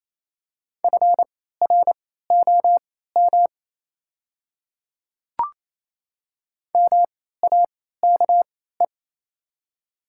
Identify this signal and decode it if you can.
Morse code, heard as a keyed tone, about 28 wpm, spaced as for 16 wpm, 690 Hz. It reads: FROM MAKE
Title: 3x Character Spacing - Top 100 Words - 28wpm